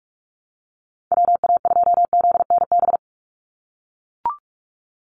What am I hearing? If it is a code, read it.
Morse code, heard as a keyed tone, about 44 wpm, 700 Hz. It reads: WA2ZNB
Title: USA Call Signs 45wpm